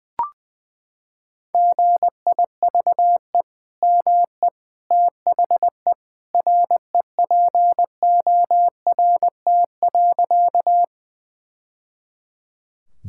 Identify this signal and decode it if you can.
Morse code, heard as a keyed tone, about 20 wpm, 700 Hz. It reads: GIVE ME THE REPORT.
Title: Sentences from Top 1000 Words - Review Part 6 - 20wpm